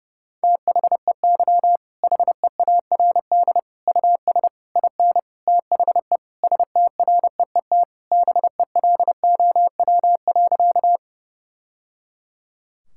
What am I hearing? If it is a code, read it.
Morse code, heard as a keyed tone, about 30 wpm, 700 Hz. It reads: THEY HEARD US IN THE STREET BELOW.
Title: Sentences from Top 1000 Words - Review Part 5 - 30wpm